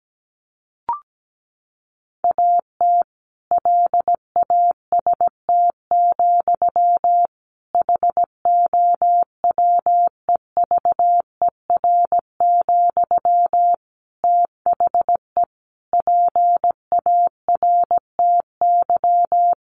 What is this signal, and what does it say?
Morse code, heard as a keyed tone, about 17 wpm, 700 Hz. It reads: AT LAST, HOWEVER, THE PARTY
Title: Sentences from Top 1400 Words 17wpm